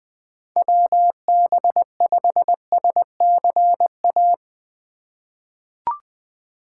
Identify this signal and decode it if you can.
Morse code, heard as a keyed tone, about 20 wpm, 700 Hz. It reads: WB5SCA
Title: USA Call Signs - 6 Characters - 3x Character Spacing - 20wpm